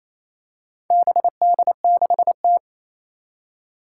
Morse code, heard as a keyed tone, about 28 wpm, 700 Hz. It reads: BD6T
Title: Four Challenging Letter-Number Combinations 28wpm